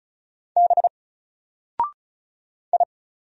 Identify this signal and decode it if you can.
Morse code, heard as a keyed tone, about 36 wpm, 700 Hz. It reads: B I
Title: Single Character (Letter:Number:Punctuation:Prosign) Alt 35wpm